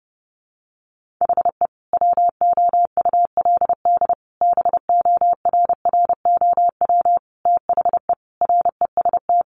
Morse code, heard as a keyed tone, about 30 wpm, 700 Hz. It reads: HE WOULD BORROW THE REST
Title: Sentences from Top 1500 Words 30wpm